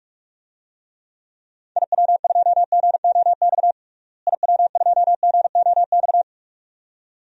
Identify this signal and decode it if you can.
Morse code, heard as a keyed tone, about 47 wpm, 700 Hz. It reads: IW2GOX IW2GOX
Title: DXCC-100 Call Signs - 6 Characters - Repeated 2x - 45wpm